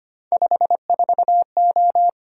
Morse code, heard as a keyed tone, about 25 wpm, 700 Hz. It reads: 54O
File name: Three Letter-Number Combinations - 25wpm